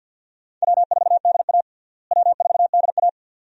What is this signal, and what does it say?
Morse code, heard as a keyed tone, about 53 wpm, 700 Hz. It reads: W4DA W4DA
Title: USA Call Signs - 4 Characters - Repeated 2x - 50wpm